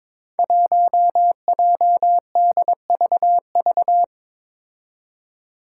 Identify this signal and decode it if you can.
Morse code, heard as a keyed tone, about 22 wpm, 700 Hz. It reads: 1JDVV
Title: Five Challenging Letter-Number Combinations 22wpm